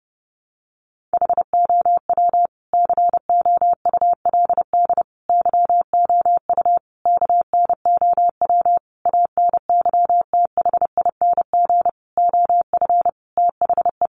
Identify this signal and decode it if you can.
Morse code, heard as a keyed tone, about 28 wpm, 700 Hz. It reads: HOW COULD YOU KNOW ANYTHING OF THE
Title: Sentences from Top 1000 Words 30wpm